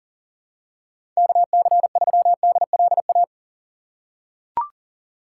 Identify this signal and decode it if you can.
Morse code, heard as a keyed tone, about 40 wpm, 700 Hz. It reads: KC3DLA
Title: USA Call Signs - 3x Character Spacing - 40wpm